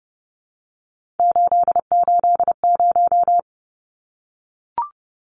Morse code, heard as a keyed tone, about 30 wpm, 700 Hz. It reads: 880
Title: Numbers - 3-Digit 30wpm